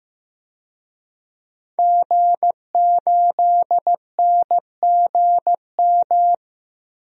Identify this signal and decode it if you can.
Morse code, heard as a keyed tone, about 15 wpm, 700 Hz. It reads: G8NGM
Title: DXCC-100 Call Signs - 5 Characters - 15wpm